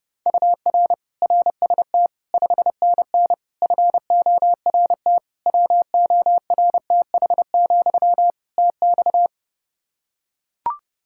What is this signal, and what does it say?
Morse code, heard as a keyed tone, about 30 wpm, 700 Hz. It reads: UR RST 5NN FORT WORTH, TX
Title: Practice QSO - Exchange 1 - Beginning - 30wpm